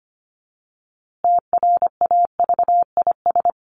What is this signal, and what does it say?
Morse code, heard as a keyed tone, about 25 wpm, 700 Hz. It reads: TRAVIS
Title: A Sound Of Thunder 25wpm